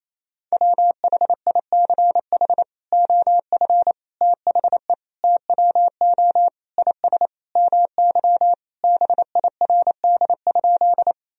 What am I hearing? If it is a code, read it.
Morse code, heard as a keyed tone, about 27 wpm, 700 Hz. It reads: WHICH OF THE TWO IS MY BIRD?
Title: Sentences from Top 1000 Words - Review Part 2 - 28wpm